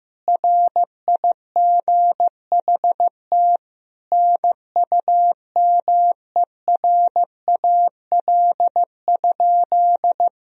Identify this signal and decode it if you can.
Morse code, heard as a keyed tone, about 15 wpm, 700 Hz. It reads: RIGHT NUMERAL?
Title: Sentences from Top 1000 Words - Review Part 2 - 15wpm